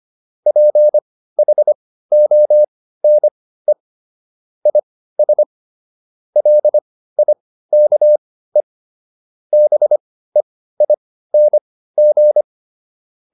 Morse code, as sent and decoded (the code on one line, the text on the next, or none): .--. .... --- -. / .. ... / .-.. .. -.- . / -... .. -. --.
PHONE IS LIKE BEING